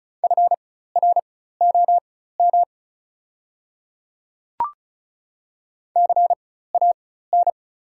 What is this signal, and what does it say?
Morse code, heard as a keyed tone, about 36 wpm, spaced as for 16 wpm, 690 Hz. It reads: FROM CAN
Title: 4x Character Spacing - Top 100 Words - 35wpm